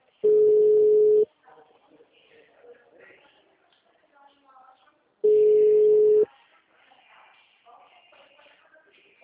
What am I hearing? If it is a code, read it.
Ringback tone (Europe, 425 Hz), with speech.
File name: out- -2022- - - .111